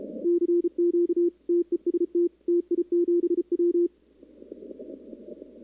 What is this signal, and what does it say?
Morse code, heard as a keyed tone, about 34 wpm, 340 Hz. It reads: CQ TEST TI7W
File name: contests / 2019-cw-wpx-kt5j-k5pi- -2330 0-11